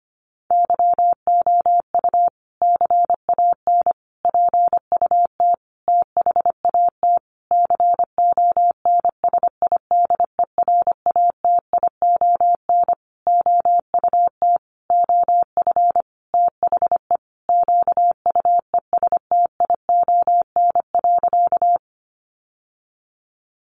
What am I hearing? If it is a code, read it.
Morse code, heard as a keyed tone, about 25 wpm, 700 Hz. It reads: YOU CAN PUT THAT CONSIDERATION OUT OF THE QUESTION.
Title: Sentences from Top 1500 Words 25wpm